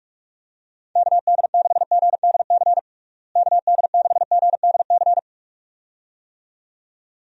Morse code, heard as a keyed tone, about 46 wpm, 700 Hz. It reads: KD6GDC KD6GDC
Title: USA Call Signs - 6 Characters - Repeated 2x - 45wpm